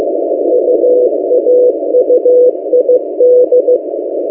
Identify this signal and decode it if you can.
Morse code, heard as a keyed tone, about 15 wpm, 500 Hz. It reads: UAUID
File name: M08a- -1355utc-8096khz